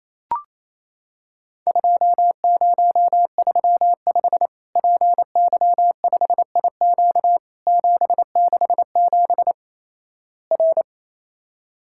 Morse code, heard as a keyed tone, about 28 wpm, 700 Hz. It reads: 2035 PY5IQ 767 R
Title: Worked All Europe DX Contest - QTC Exchange - 28wpm